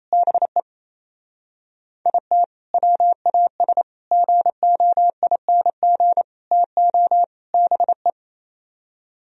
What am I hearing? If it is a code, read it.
Morse code, heard as a keyed tone, about 36 wpm, 700 Hz. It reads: BE IT WAS GOING TO BE